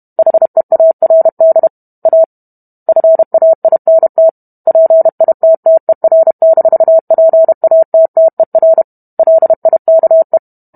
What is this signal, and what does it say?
Morse code, heard as a keyed tone, about 32 wpm, 650 Hz. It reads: HEARD A FAINT PITTER-PATTER LIKE
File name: Book Two / 05-Chapter 5 - 31 WPM